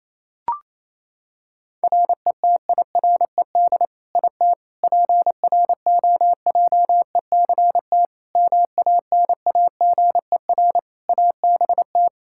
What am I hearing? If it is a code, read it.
Morse code, heard as a keyed tone, about 28 wpm, 700 Hz. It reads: RETIRED IT PROJECT MANAGER ABT
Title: Practice QSO - Exchange 3 - More Info - 28wpm